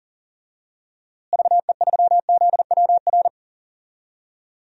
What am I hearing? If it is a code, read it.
Morse code, heard as a keyed tone, about 40 wpm, 700 Hz. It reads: VE3ZWR
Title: DXCC-100 Call Signs - 6 Characters - Repeated 3x - 40wpm